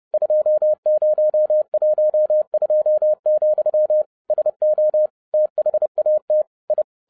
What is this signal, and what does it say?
Morse code, heard as a keyed tone, about 30 wpm, 610 Hz. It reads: 2012, SO THAT I
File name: 2012-01-mp3 / n12-I DIDNT MISS THE SOLAR CYCLE 24-30wpm